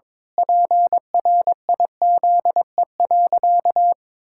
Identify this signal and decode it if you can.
Morse code, heard as a keyed tone, about 22 wpm, 700 Hz. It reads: PRIZE.
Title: Pangram Challenge v2 - 22wpm